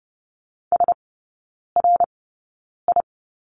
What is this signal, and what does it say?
Morse code, heard as a keyed tone, about 30 wpm, spaced as for 9 wpm, 700 Hz. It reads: SRI
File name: Lesson 17e - Letter u - Three Characters 30wpm